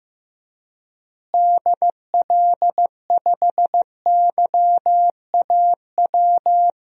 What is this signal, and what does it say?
Morse code, heard as a keyed tone, about 15 wpm, 700 Hz. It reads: DL5YAW
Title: DXCC-100 Call Signs - 6 Characters - Repeated 3x - 15wpm